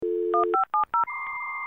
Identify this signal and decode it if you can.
DTMF, heard as telephone touch tones, tones of about 120 ms. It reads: 1570